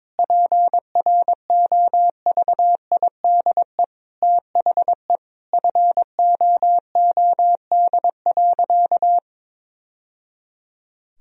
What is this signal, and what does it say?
Morse code, heard as a keyed tone, about 22 wpm, 700 Hz. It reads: PROVIDE THE FOOD.
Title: Sentences from Top 1000 Words - Review Part 3 - 22wpm